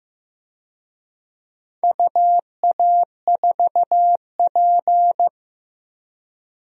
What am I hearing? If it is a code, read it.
Morse code, heard as a keyed tone, about 15 wpm, 700 Hz. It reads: UA4P
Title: DXCC-100 Call Signs - 4 Characters - 15wpm